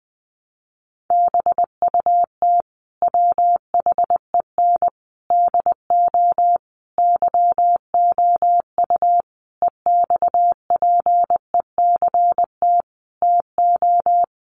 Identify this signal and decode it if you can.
Morse code, heard as a keyed tone, about 20 wpm, 700 Hz. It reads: BUT WHEN DO YOU EXPECT TO
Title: Sentences from Top 1100 Words 20wpm